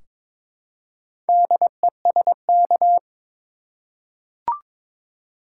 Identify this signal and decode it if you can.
Morse code, heard as a keyed tone, about 22 wpm, 700 Hz. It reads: DESK
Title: Four Letter Words - 22wpm